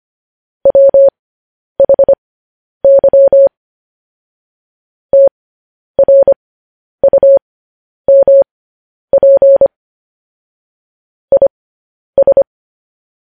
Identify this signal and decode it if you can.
Morse code, heard as a keyed tone, about 25 wpm, spaced as for 10 wpm, 550 Hz. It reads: WHY TRUMP IS
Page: News Headlines in Morse Code at 10 WPM